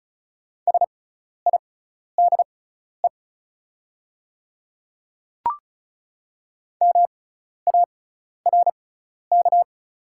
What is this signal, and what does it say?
Morse code, heard as a keyed tone, about 33 wpm, spaced as for 12 wpm, 690 Hz. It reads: SIDE MARK
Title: 6x Character Spacing - Top 500 Words - 35wpm